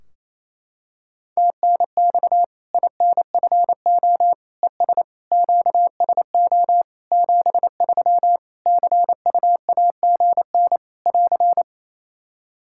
Morse code, heard as a keyed tone, about 27 wpm, 700 Hz. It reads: TNX INFO ES QSO 73 CUAGN <AR>
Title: Practice QSO - Exchange 4 - Ending - 28wpm